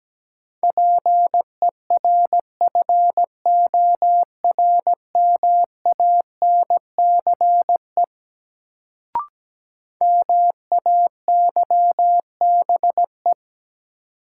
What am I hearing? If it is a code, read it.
Morse code, heard as a keyed tone, about 17 wpm, 700 Hz. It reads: PERFORMANCE MAYBE
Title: Sentences from Top 1000 Words - New Word Drill - 17wpm